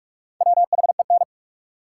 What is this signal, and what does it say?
Morse code, heard as a keyed tone, about 48 wpm, 700 Hz. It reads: WHEN
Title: Sets of 2 Words - New Word Drill - 45wpm